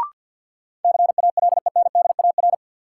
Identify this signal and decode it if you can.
Morse code, heard as a keyed tone, about 52 wpm, 700 Hz. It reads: CALENDAR
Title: Sentences from Top 1600 Words - New Word Drill - 50wpm